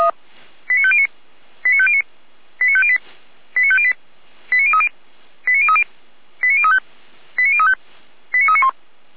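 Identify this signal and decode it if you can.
Five-tone selective call (ZVEI), heard as quick runs of five tones, tones of about 70 ms. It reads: 89509 89509 89508 89508 89030 89030 89035 89035 89391
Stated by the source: ZVEI five-tone selective call received with an SDR.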